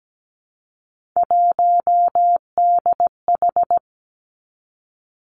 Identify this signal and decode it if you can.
Morse code, heard as a keyed tone, about 17 wpm, 700 Hz. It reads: 1DH
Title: Three Challenging Letter-Number Combinations 17wpm